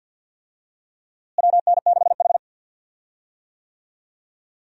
Morse code, heard as a keyed tone, about 51 wpm, 700 Hz. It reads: WN6H
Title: USA Call Signs - 4 Characters - Repeated 3x - 50wpm